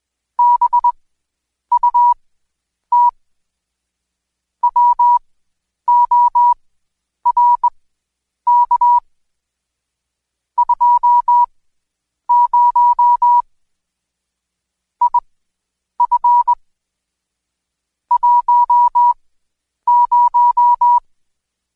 Morse code, heard as a keyed tone, about 20 wpm, spaced as for 9 wpm, 970 Hz. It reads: BUT WORK 20 IF 10
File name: Code Quick / AudioDownld2 / Seg116-Day21-DayDrill 21